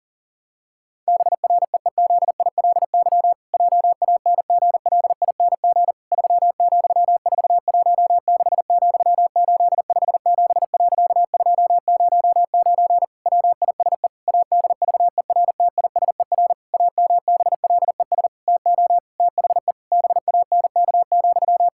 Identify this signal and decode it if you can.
Morse code, heard as a keyed tone, about 40 wpm, 700 Hz. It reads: BREEZILY JANGLING 3,416,857.209 WISE ADVERTISER AMBLES TO THE BANK,
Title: Pangram Challenge v2 - 40wpm